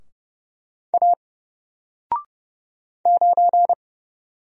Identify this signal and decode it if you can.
Morse code, heard as a keyed tone, about 30 wpm, 700 Hz. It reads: A 9
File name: Single-Letter-Number-30wpm